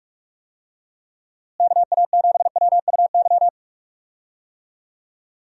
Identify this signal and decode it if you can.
Morse code, heard as a keyed tone, about 46 wpm, 700 Hz. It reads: KA7WUY
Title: USA Call Signs - 6 Characters - Repeated 2x - 45wpm